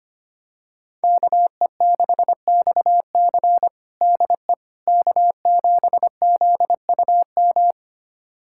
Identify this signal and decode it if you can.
Morse code, heard as a keyed tone, about 25 wpm, 700 Hz. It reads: KE6XC DE K7ZUM
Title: USA Callsign DE Callsign - 5 Characters - 25wpm